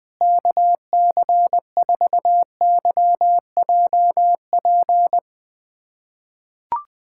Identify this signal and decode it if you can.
Morse code, heard as a keyed tone, about 20 wpm, 700 Hz. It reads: KC4YJP
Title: USA Callsign DE Callsign 20wpm